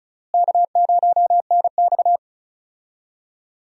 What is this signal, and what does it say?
Morse code, heard as a keyed tone, about 36 wpm, 700 Hz. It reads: K0NX
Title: USA Call Signs - 4 Characters - Repeated 2x - 35wpm